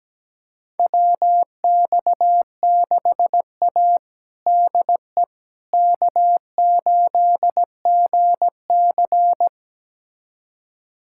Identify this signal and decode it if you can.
Morse code, heard as a keyed tone, about 17 wpm, 700 Hz. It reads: WX6A DE K8GC
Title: USA Callsign DE Callsign - 4 Characters - 17wpm